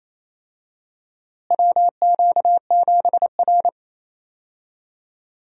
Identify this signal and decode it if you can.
Morse code, heard as a keyed tone, about 28 wpm, 700 Hz. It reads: WQ7R
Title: USA Call Signs - 4 Characters - Repeated 3x - 28wpm